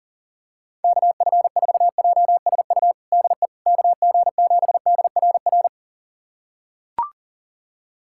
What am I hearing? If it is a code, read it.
Morse code, heard as a keyed tone, about 40 wpm, 700 Hz. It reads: KF4JSU DE KG7DRR
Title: USA Callsign DE Callsign 40wpm